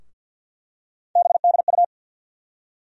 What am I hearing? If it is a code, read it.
Morse code, heard as a keyed tone, about 52 wpm, 700 Hz. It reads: BDU
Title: USA Call Sign Postfixes - 3 Letters 50wpm